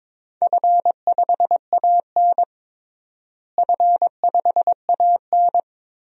Morse code, heard as a keyed tone, about 22 wpm, 700 Hz. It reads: F5AN F5AN
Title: DXCC-100 Call Signs - 4 Characters - Repeated 3x - 22wpm